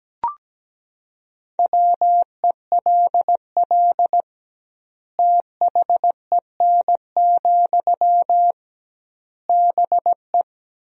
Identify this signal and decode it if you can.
Morse code, heard as a keyed tone, about 17 wpm, 700 Hz. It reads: WELL THEN, BE
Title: 2x Word-Spacing - Sentences from Top 500 Words - 17wpm